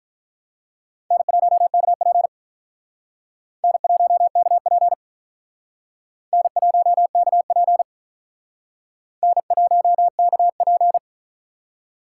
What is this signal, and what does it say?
Morse code, heard as a keyed tone, about 44 wpm, 700 Hz. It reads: N1KP N1KP N1KP N1KP